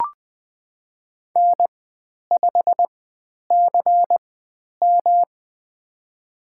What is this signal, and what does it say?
Morse code, heard as a keyed tone, about 20 wpm, spaced as for 10 wpm, 700 Hz. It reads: N5CM
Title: 109 Lesson 18g - Letter m - Call Signs 20wpm